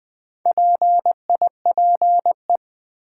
Morse code, heard as a keyed tone, about 20 wpm, 700 Hz. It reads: PIPE
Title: Sentences from Top 1500 Words - New Word Drill - 20wpm